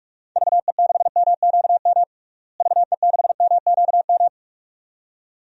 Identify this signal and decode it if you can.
Morse code, heard as a keyed tone, about 46 wpm, 700 Hz. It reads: VE6MQM VE6MQM
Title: DXCC-100 Call Signs - 6 Characters - Repeated 2x - 45wpm